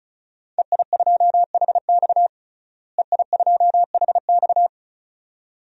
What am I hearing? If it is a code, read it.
Morse code, heard as a keyed tone, about 34 wpm, 700 Hz. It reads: EI2HX EI2HX
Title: DXCC-100 Call Signs - 5 Characters - Repeated 2x - 35wpm